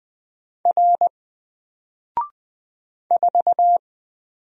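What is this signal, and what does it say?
Morse code, heard as a keyed tone, about 20 wpm, 700 Hz. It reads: R 4
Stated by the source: Single letter and number Morse drill at 20 wpm, from N4LCD.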